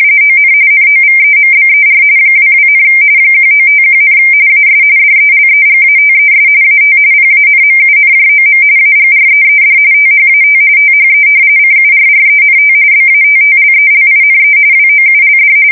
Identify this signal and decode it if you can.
RTTY, about 45.45 baud. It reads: ON NOVEMBER 22, 1951, AS DELEGATES AT PANMUNJOM AGREED THAT THE CURRENT BATTLELINE WOULD
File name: fsk170 / battleshipnj